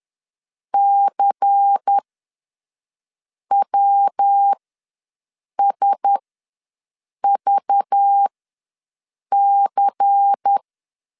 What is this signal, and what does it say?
Morse code, heard as a keyed tone, about 10 wpm, spaced as for 5 wpm, 800 Hz. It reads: CWSVC
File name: LEZIONI / 5 LBVC / 5 LBVC ws 10wpm 3733